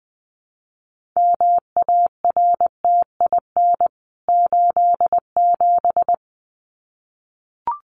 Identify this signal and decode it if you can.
Morse code, heard as a keyed tone, about 20 wpm, 700 Hz. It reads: MARTIN 87
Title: CWT Exchange 20wpm